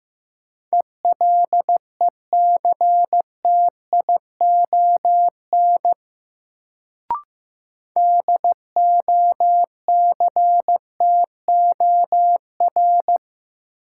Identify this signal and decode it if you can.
Morse code, heard as a keyed tone, about 15 wpm, 700 Hz. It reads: ELECTION DOCTOR
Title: Sentences from Top 1500 Words - New Word Drill - 15wpm